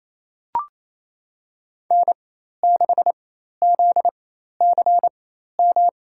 Morse code, heard as a keyed tone, about 27 wpm, spaced as for 13 wpm, 700 Hz. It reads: N6ZCM
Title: USA Call Signs - 5 Characters - 4x Character Spacing - 28wpm